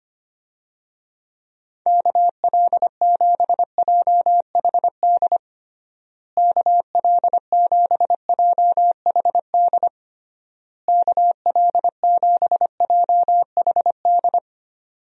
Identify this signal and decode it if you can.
Morse code, heard as a keyed tone, about 25 wpm, 700 Hz. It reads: KL7JHD KL7JHD KL7JHD
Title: USA Call Signs - 6 Characters - Repeated 3x - 25wpm